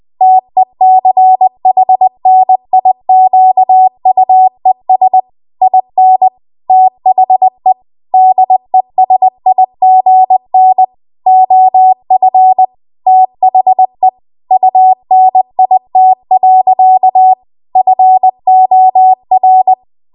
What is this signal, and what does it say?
Morse code, heard as a keyed tone, about 20 wpm, 750 Hz. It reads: TECHNIQUES IN THE DESIGN OF THE UNIT. FOR
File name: Archive / 20 WPM / 20WPM